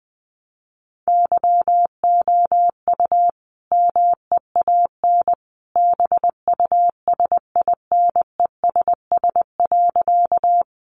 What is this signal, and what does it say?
Morse code, heard as a keyed tone, about 20 wpm, 700 Hz. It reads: YOU MEAN BUSINESS.
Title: Sentences from Top 600 Words 20wpm